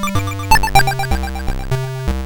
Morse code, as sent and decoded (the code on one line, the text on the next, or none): none